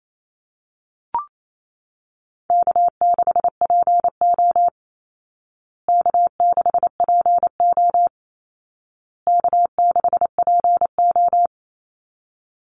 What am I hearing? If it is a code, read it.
Morse code, heard as a keyed tone, about 28 wpm, 700 Hz. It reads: K6PO K6PO K6PO
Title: Call Signs Repeated 2x - 28wpm